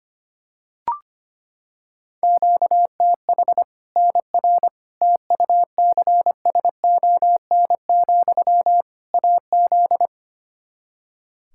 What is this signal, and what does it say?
Morse code, heard as a keyed tone, about 25 wpm, 700 Hz. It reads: QTH NR TUCSON, AZ
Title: K1USN SST QSO Practice 25wpm